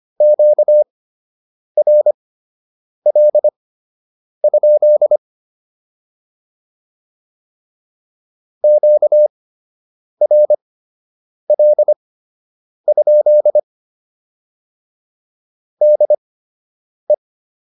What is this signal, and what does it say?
Morse code, heard as a keyed tone, about 25 wpm, spaced as for 8 wpm, 600 Hz. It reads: QRL? QRL? DE